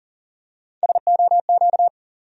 Morse code, heard as a keyed tone, about 39 wpm, 700 Hz. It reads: SOQ
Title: USA Call Sign Postfixes - 3 Letters 40wpm